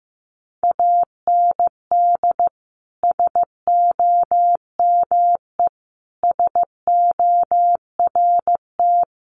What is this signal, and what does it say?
Morse code, heard as a keyed tone, about 15 wpm, 700 Hz. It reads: AND SOME SORT
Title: Sentences from Top 1300 Words 15wpm